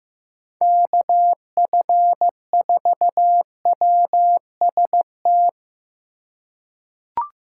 Morse code, heard as a keyed tone, about 15 wpm, 700 Hz. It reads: KF4WST
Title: USA Call Signs - 6 Characters - 3x Character Spacing - 15wpm